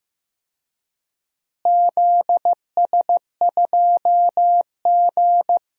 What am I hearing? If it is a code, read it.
Morse code, heard as a keyed tone, about 15 wpm, 700 Hz. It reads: ZS2G
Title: DXCC-100 Call Signs - 4 Characters - Repeated 3x - 15wpm